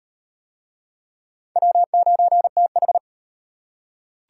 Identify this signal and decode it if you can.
Morse code, heard as a keyed tone, about 33 wpm, 700 Hz. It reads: W9TH